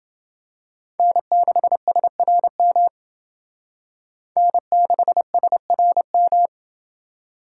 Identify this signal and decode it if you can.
Morse code, heard as a keyed tone, about 28 wpm, 700 Hz. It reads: N6SRM N6SRM